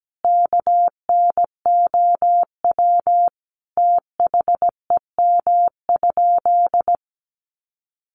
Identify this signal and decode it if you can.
Morse code, heard as a keyed tone, about 17 wpm, 700 Hz. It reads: KNOW THEM?
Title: Sentences from Top 100 Words 17wpm